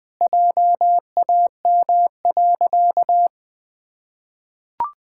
Morse code, heard as a keyed tone, about 21 wpm, 700 Hz. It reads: JAM.
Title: Pangram Challenge v2 - 20wpm